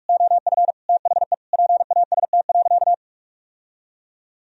Morse code, heard as a keyed tone, about 46 wpm, 700 Hz. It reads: OF THE PAST.
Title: Sentences from Top 1000 Words - Review Part 3 - 45wpm